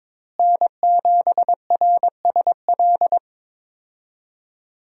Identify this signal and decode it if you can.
Morse code, heard as a keyed tone, about 23 wpm, 700 Hz. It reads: N7RSL
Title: USA Call Signs - 5 Characters - Repeated 3x - 22wpm